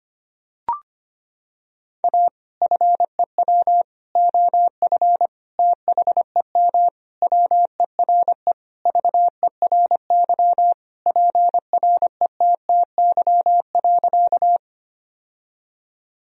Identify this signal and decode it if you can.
Morse code, heard as a keyed tone, about 25 wpm, 700 Hz. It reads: A FEW OF THEM WERE VERY PRETTY.
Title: Sentences from Top 1000 Words - Review Part 1 - 25wpm